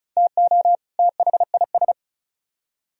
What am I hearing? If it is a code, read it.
Morse code, heard as a keyed tone, about 45 wpm, 700 Hz. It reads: TO THIS